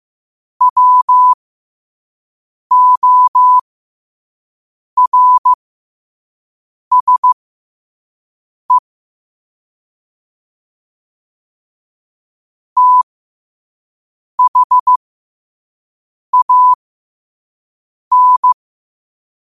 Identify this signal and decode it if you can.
Morse code, heard as a keyed tone, about 15 wpm, spaced as for 5 wpm, 1000 Hz. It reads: WORSE THAN
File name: qotdfiles / 5wpm